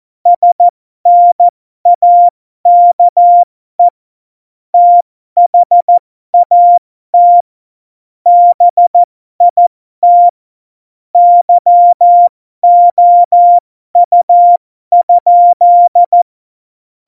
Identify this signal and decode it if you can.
Morse code, heard as a keyed tone, about 14 wpm, spaced as for 12 wpm, 700 Hz. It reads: SNAKE THAT BIT YOU?
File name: Morsekurs / Witze